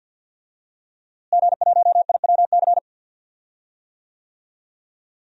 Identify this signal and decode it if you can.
Morse code, heard as a keyed tone, about 52 wpm, 700 Hz. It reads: G1IWC
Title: DXCC-100 Call Signs - 5 Characters - Repeated 3x - 50wpm